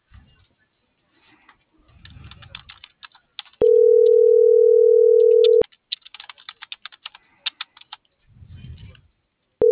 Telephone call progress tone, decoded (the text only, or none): ringback tone